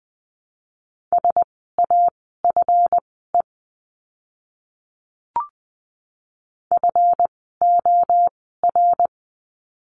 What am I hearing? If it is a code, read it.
Morse code, heard as a keyed tone, about 19 wpm, spaced as for 14 wpm, 700 Hz. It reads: SAFE FOR
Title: 139 Lesson 23f - Letter f - Words 20wpm